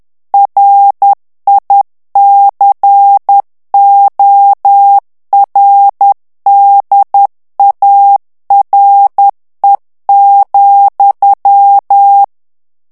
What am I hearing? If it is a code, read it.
Morse code, heard as a keyed tone, about 10 wpm, 800 Hz. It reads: RICORDARE,
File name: morse mp3 / ITALIEN texte clair 1 vitesse 10 mots minute